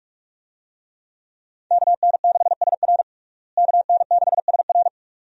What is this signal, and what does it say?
Morse code, heard as a keyed tone, about 43 wpm, 700 Hz. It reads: KN6SR KN6SR
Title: USA Call Signs - 5 Characters - Repeated 2x - 45wpm